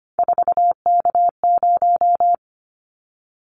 Morse code, heard as a keyed tone, about 25 wpm, 700 Hz. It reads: 4K0
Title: Callsign Prefixes for Top 100 DXCC - 25wpm